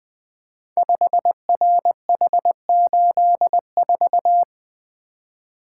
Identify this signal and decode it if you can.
Morse code, heard as a keyed tone, about 20 wpm, 700 Hz. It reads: 5RH84
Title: Code Groups - 5 Letter:Numbers - 20wpm